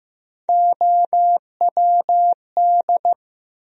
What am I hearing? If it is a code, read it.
Morse code, heard as a keyed tone, about 15 wpm, 700 Hz. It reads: OWD
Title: USA Call Sign Postfixes - 3 Letters 15wpm